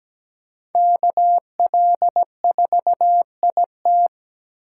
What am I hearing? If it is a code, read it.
Morse code, heard as a keyed tone, about 20 wpm, 700 Hz. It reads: KL4IT
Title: Speed Racing - USA 2x2 Call Signs 17wpm